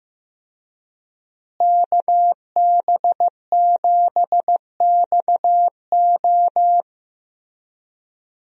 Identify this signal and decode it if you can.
Morse code, heard as a keyed tone, about 15 wpm, 700 Hz. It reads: KB7XO
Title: USA Call Signs - 5 Characters - Repeated 3x - 15wpm